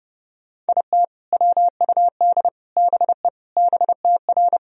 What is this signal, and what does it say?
Morse code, heard as a keyed tone, about 32 wpm, 700 Hz. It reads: IT WUD BE BTR
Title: Sentences from Top 400 Words and Abbreviations 30wpm